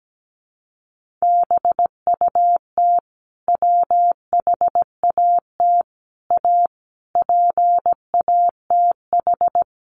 Morse code, heard as a keyed tone, about 17 wpm, 700 Hz. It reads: BUT WHAT A PATH
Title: Sentences from Top 1400 Words 17wpm